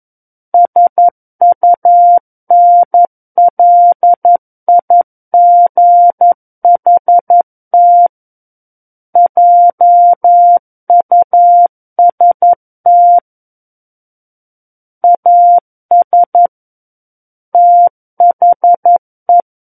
Morse code, heard as a keyed tone, about 11 wpm, 700 Hz. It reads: SUNLIGHT JUST AS THE
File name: Book One / 02-Chapter 2 - 11 WPM